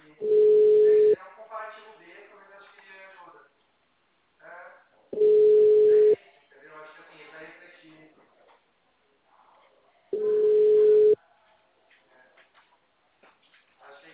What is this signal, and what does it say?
Ringback tone (Europe, 425 Hz), with speech.